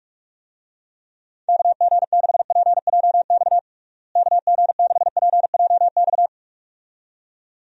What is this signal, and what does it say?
Morse code, heard as a keyed tone, about 42 wpm, 700 Hz. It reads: KG6PJX KG6PJX
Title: USA Call Signs - 6 Characters - Repeated 2x - 45wpm